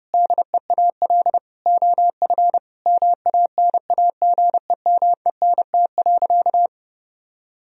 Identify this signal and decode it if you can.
Morse code, heard as a keyed tone, about 30 wpm, 700 Hz. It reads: DEAL OF MANAGEMENT.
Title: Sentences from Top 1000 Words - Review Part 4 - 30wpm